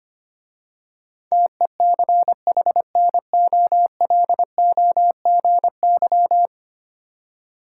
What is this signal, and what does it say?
Morse code, heard as a keyed tone, about 25 wpm, 690 Hz. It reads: TECHNOLOGY
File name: 10 Letter Words - First 3 Letters Given - 25wpm